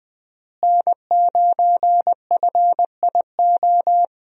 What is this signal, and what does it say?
Morse code, heard as a keyed tone, about 20 wpm, 700 Hz. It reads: N9FIO
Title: USA Call Signs - 5 Characters - Repeated 3x - 20wpm